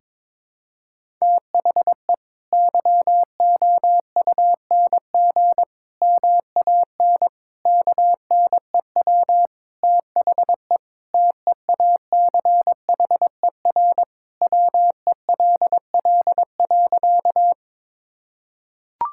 Morse code, heard as a keyed tone, about 22 wpm, 700 Hz. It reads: THE YOUNG MAN KNEW THE TEACHER WELL.
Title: Sentences from Top 1000 Words - Review Part 6 - 22wpm